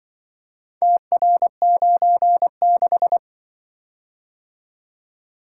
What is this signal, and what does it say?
Morse code, heard as a keyed tone, about 25 wpm, 700 Hz. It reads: TR96
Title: Speed-Racing - Four Letter-Number Combinations - 20wpm